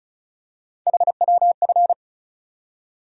Morse code, heard as a keyed tone, about 34 wpm, 700 Hz. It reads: HWF
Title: USA Call Sign Postfixes - 3 Letters 35wpm